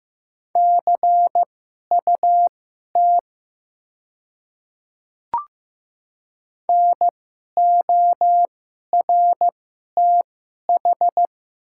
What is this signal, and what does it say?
Morse code, heard as a keyed tone, about 15 wpm, spaced as for 11 wpm, 700 Hz. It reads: CUT NORTH
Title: 2x Character Spacing - Top 500 Words - 15wpm